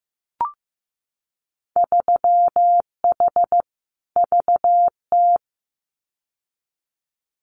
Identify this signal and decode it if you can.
Morse code, heard as a keyed tone, about 15 wpm, 700 Hz. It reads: 3H VT
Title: Winter Field Day Exchanges Encore 15wpm